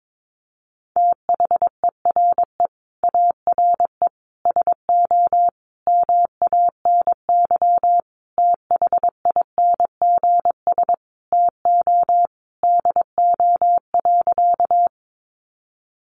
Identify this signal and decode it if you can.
Morse code, heard as a keyed tone, about 22 wpm, 700 Hz. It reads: THERE ARE SO MANY THINGS TO DO.
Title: Additional Easy Phrases 22wpm